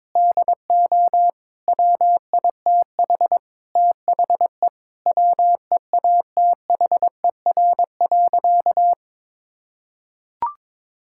Morse code, heard as a keyed tone, about 22 wpm, 700 Hz. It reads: DO WITH THE WEATHER.
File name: Sentences with 9-Word Phrases - 22wpm